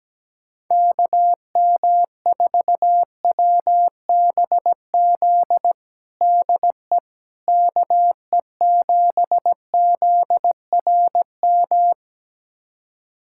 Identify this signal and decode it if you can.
Morse code, heard as a keyed tone, about 17 wpm, 700 Hz. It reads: KM4WBZ DE KE7ZRM
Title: USA Callsign DE Callsign 17wpm